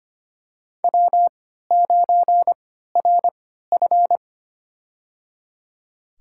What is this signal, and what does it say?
Morse code, heard as a keyed tone, about 25 wpm, spaced as for 14 wpm, 700 Hz. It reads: W9RF
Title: USA Call Signs - 4 Characters - 3x Character Spacing - 25wpm